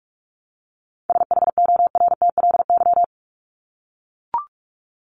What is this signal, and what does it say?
Morse code, heard as a keyed tone, about 44 wpm, 700 Hz. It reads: SHORTLY
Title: LY Suffix - Top 100 Words with Base Words 45wpm